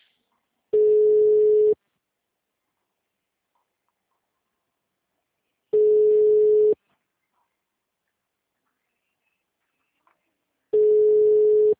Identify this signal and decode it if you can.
Ringback tone (Europe, 425 Hz).